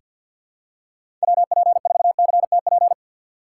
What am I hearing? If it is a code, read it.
Morse code, heard as a keyed tone, about 47 wpm, 690 Hz. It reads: WP4CTP
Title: USA Call Signs - 6x Character Spacing - 50wpm